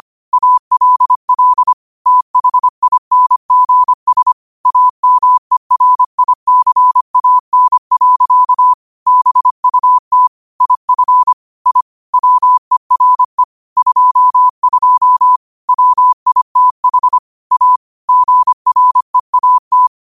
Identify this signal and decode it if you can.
Morse code, heard as a keyed tone, about 25 wpm, 1000 Hz. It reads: ALL THINGS AMERICAN. BUT IF I WERE 22 WITH A GREAT
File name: qotdfiles / 25wpm